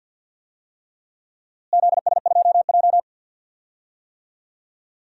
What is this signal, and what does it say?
Morse code, heard as a keyed tone, about 49 wpm, 700 Hz. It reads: ZS2J
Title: DXCC-100 Call Signs - 4 Characters - Repeated 3x - 50wpm